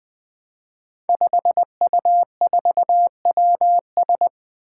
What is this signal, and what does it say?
Morse code, heard as a keyed tone, about 20 wpm, 700 Hz. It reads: 5U4WS
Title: Five Challenging Letter-Number Combinations 20wpm